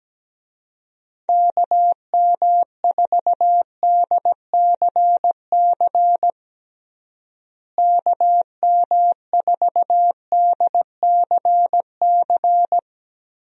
Morse code, heard as a keyed tone, about 17 wpm, 700 Hz. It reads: KM4DCC KM4DCC
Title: USA Call Signs - 6 Characters - Repeated 3x - 17wpm